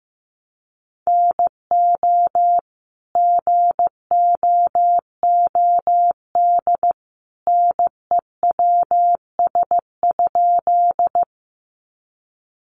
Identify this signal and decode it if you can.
Morse code, heard as a keyed tone, about 15 wpm, 700 Hz. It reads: NO GOOD NEWS?
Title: Sentences from Top 800 Words 15wpm